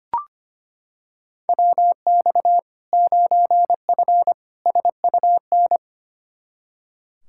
Morse code, heard as a keyed tone, about 25 wpm, 700 Hz. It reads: WX 9F SUN
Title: Practice QSO - Exchange 2 - Info - 25wpm